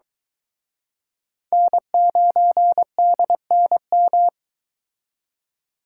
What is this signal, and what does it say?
Morse code, heard as a keyed tone, about 22 wpm, 700 Hz. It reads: N9DNM